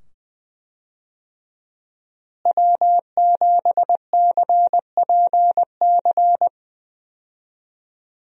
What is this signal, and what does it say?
Morse code, heard as a keyed tone, about 20 wpm, 700 Hz. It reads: W7CPC
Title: USA Call Signs - 5 Characters - Repeated 2x - 20wpm